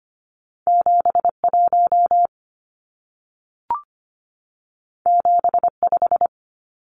Morse code, heard as a keyed tone, about 25 wpm, 700 Hz. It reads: 71 75
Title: Numbers - 2-Digit 25wpm